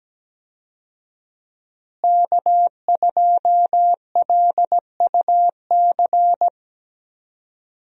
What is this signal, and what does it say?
Morse code, heard as a keyed tone, about 17 wpm, 700 Hz. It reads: K2LUC
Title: USA Call Signs - 5 Characters - Repeated 3x - 17wpm